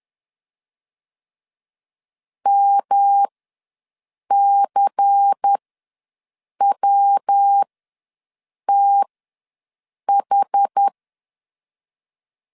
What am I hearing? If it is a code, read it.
Morse code, heard as a keyed tone, about 10 wpm, spaced as for 6 wpm, 800 Hz. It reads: MCWTH
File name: LEZIONI / 6 PKX0 / 6 PK ws 10wpm 3745